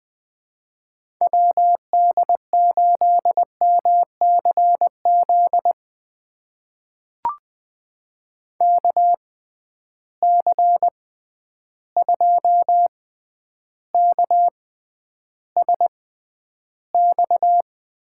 Morse code, heard as a keyed tone, about 21 wpm, spaced as for 10 wpm, 700 Hz. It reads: WD8MCZ KC2KSX
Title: USA Call Signs - 6 Characters - 6x Character Spacing - 20wpm